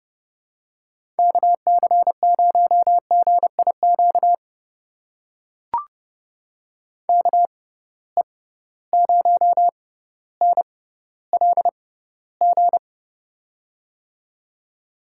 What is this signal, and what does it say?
Morse code, heard as a keyed tone, about 30 wpm, spaced as for 15 wpm, 700 Hz. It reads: KC0GIQ KE0NLG
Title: USA Call Signs - 6 Characters - 6x Character Spacing - 30wpm